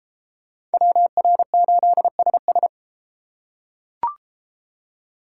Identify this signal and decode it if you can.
Morse code, heard as a keyed tone, about 32 wpm, 700 Hz. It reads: WR8SS